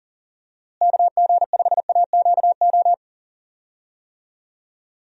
Morse code, heard as a keyed tone, about 40 wpm, 700 Hz. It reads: KG5AQO
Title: USA Call Signs - 6 Characters - Repeated 3x - 40wpm